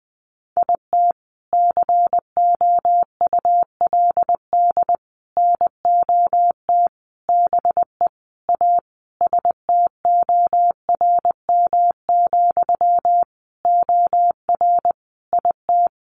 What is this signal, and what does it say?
Morse code, heard as a keyed tone, about 20 wpm, 700 Hz. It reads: IT COULD NOT BE A STORM, OR IT
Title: Sentences from Top 1500 Words 20wpm